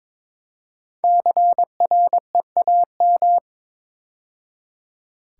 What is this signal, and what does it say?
Morse code, heard as a keyed tone, about 22 wpm, 700 Hz. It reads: CREAM
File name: 5 Letter Words - First 3 Letters Given - 22wpm